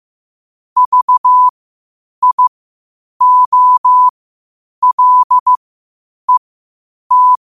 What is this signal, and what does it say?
Morse code, heard as a keyed tone, about 15 wpm, spaced as for 8 wpm, 1000 Hz. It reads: VIOLET